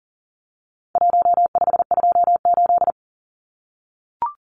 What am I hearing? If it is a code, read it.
Morse code, heard as a keyed tone, about 40 wpm, 700 Hz. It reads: 1528
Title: Numbers - Contest Style - 4-Digit 40wpm